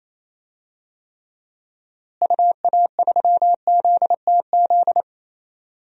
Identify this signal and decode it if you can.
Morse code, heard as a keyed tone, about 28 wpm, 700 Hz. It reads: UA3ZTZ
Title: DXCC-100 Call Signs - 6 Characters - Repeated 2x - 28wpm